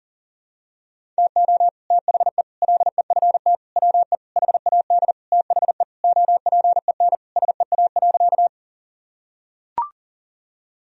Morse code, heard as a keyed tone, about 40 wpm, 700 Hz. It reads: TO THE LEFT WE HAD THE OPEN SEA.
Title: Sentences from Top 1000 Words - Review Part 1 - 40wpm